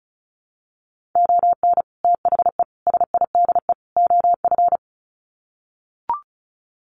Morse code, heard as a keyed tone, about 34 wpm, 700 Hz. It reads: ON THE SIDE OF
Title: Sets of 4 Words 35wpm